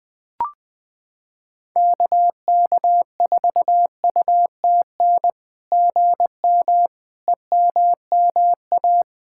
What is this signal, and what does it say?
Morse code, heard as a keyed tone, about 20 wpm, 700 Hz. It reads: KK4UTN GM EMMA
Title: SOTA QSO Practice 20wpm